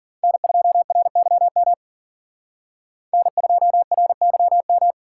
Morse code, heard as a keyed tone, about 41 wpm, 700 Hz. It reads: N2RYM N2RYM
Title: Speed Racing - USA 1x3 Call Signs 40wpm